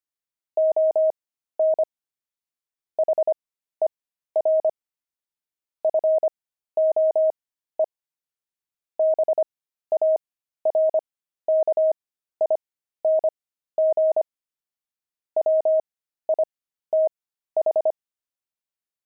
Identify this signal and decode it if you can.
Morse code, heard as a keyed tone, about 25 wpm, spaced as for 13 wpm, 630 Hz. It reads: ON HER FOE BARKING WITH
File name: pangram-fs-13-cs-25-barking